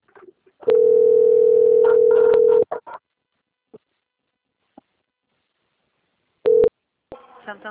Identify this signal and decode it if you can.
Ringback tone (North America), 440 + 480 Hz.